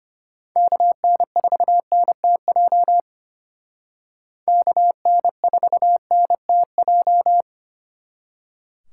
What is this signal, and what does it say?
Morse code, heard as a keyed tone, about 32 wpm, 700 Hz. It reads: KN4NTJ KN4NTJ